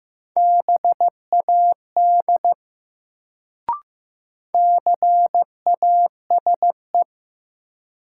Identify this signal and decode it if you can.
Morse code, heard as a keyed tone, about 15 wpm, 700 Hz. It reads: BAD CASE
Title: Sentences from Top 700 Words - New Word Drill - 15wpm